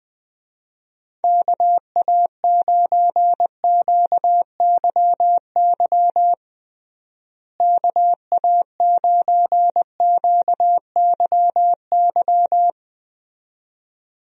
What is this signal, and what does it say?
Morse code, heard as a keyed tone, about 20 wpm, 700 Hz. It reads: KA9QYY KA9QYY
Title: USA Call Signs - 6 Characters - Repeated 2x - 20wpm